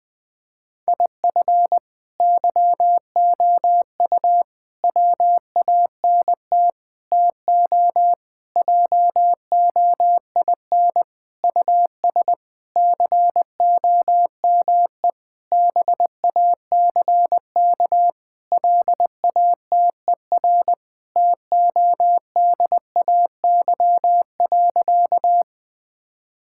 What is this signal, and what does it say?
Morse code, heard as a keyed tone, about 20 wpm, 700 Hz. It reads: IF YOU WANT TO JOIN US COME BACK LATER TODAY.
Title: Sentences with 4-Word Phrases - 20wpm